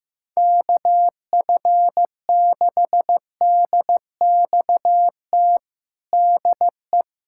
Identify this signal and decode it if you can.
Morse code, heard as a keyed tone, about 15 wpm, 700 Hz. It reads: KF6DXT DE
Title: USA Callsign DE Callsign - 6 Characters - 15wpm